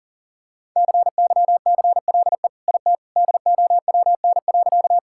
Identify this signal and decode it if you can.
Morse code, heard as a keyed tone, about 40 wpm, 700 Hz. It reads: CYCLE IT DOWN.
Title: Sentences from Top 1000 Words - Review Part 6 - 40wpm